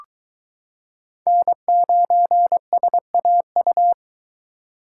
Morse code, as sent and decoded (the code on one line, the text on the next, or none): -. ----. ... .- ..-
N9SAU